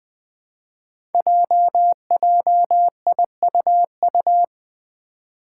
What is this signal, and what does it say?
Morse code, heard as a keyed tone, about 20 wpm, 700 Hz. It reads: JJIUU
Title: Five Challenging Letter-Number Combinations 20wpm